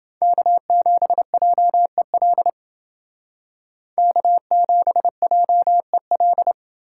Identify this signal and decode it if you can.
Morse code, heard as a keyed tone, about 29 wpm, 700 Hz. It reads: K7JEL K7JEL